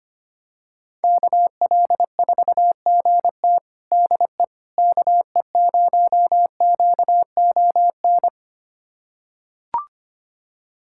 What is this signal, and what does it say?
Morse code, heard as a keyed tone, about 25 wpm, 700 Hz. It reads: KL4GT DE KE0QON
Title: USA Callsign DE Callsign 25wpm